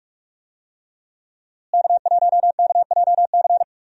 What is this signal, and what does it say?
Morse code, heard as a keyed tone, about 43 wpm, 700 Hz. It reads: K1KJC
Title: USA Call Signs - 5 Characters - Repeated 3x - 45wpm